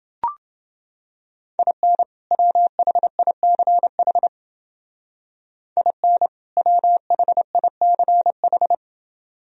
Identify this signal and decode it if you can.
Morse code, heard as a keyed tone, about 29 wpm, 700 Hz. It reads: IN WHICH IN WHICH